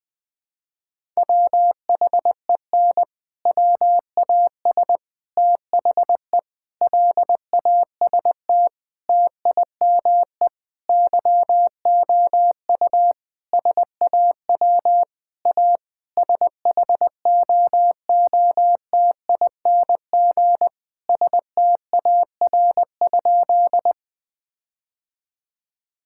Morse code, heard as a keyed tone, about 21 wpm, 700 Hz. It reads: WHEN WAS THE LAST TIME YOU SAW A SHOOTING STAR?
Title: Sentences with 8-Word Phrases - Spoken Beginning - 20wpm